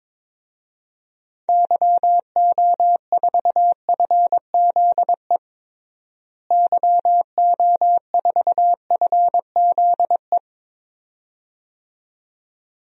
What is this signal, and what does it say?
Morse code, heard as a keyed tone, about 22 wpm, 700 Hz. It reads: YO4FZE YO4FZE
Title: DXCC-100 Call Signs - 6 Characters - Repeated 2x - 22wpm